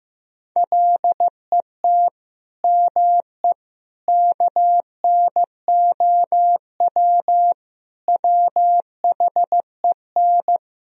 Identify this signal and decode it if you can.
Morse code, heard as a keyed tone, about 16 wpm, 700 Hz. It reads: LET ME KNOW WHEN
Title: Sentences from Top 1000 Words - Review Part 2 - 15wpm